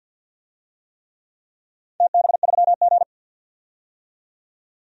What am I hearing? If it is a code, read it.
Morse code, heard as a keyed tone, about 48 wpm, 700 Hz. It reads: TB3G
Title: DXCC-100 Call Signs - 4 Characters - 50wpm